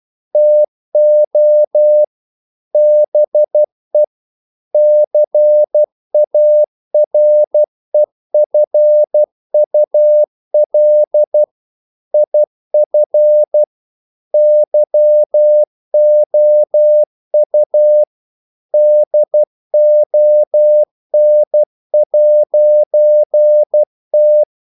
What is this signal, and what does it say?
Morse code, heard as a keyed tone, about 12 wpm, 600 Hz. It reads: TO BE CAREFUL IF YOU DON'T
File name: CW / Practice / EnglishQuotations0312